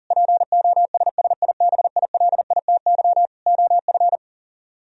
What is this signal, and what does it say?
Morse code, heard as a keyed tone, about 52 wpm, 700 Hz. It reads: POSSIBILITY OF